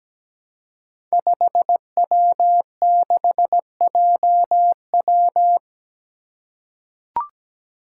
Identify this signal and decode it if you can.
Morse code, heard as a keyed tone, about 17 wpm, 700 Hz. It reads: 5W6JW
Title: Five Challenging Letter-Number Combinations 17wpm